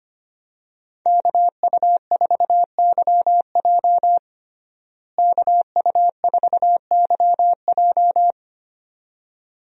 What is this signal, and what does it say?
Morse code, heard as a keyed tone, about 25 wpm, 700 Hz. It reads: KU4YJ KU4YJ
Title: USA Call Signs - 5 Characters - Repeated 2x - 25wpm